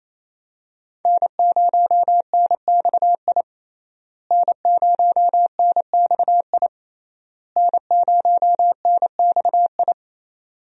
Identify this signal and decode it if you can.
Morse code, heard as a keyed tone, about 28 wpm, 700 Hz. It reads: N0NXI N0NXI N0NXI
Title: USA Call Signs - 5 Characters - Repeated 3x - 28wpm